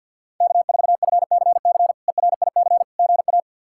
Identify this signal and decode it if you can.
Morse code, heard as a keyed tone, about 51 wpm, 700 Hz. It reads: K4FCC ERIC GA
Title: K1USN SST QSO Practice - 50wpm